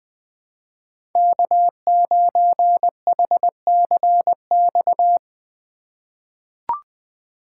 Morse code, heard as a keyed tone, about 20 wpm, 700 Hz. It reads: K9HCX
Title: USA Call Signs - 3x Character Spacing - 20wpm